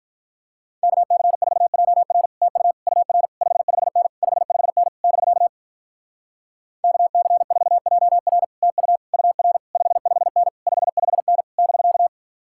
Morse code, heard as a keyed tone, about 45 wpm, 700 Hz. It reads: KC4JR TU UR 55N 55N <BK> KC4JR TU UR 55N 55N <BK>